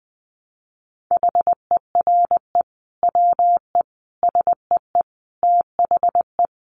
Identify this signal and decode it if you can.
Morse code, heard as a keyed tone, about 20 wpm, 700 Hz. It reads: HERE WE SEE THE
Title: Sentences from Top 300 Words 20wpm